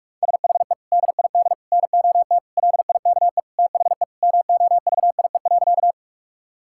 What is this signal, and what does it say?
Morse code, heard as a keyed tone, about 41 wpm, 700 Hz. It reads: SHE DID NOT LIKE THE MOVIE.
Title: Sentences from Top 1000 Words - Review Part 5 - 45wpm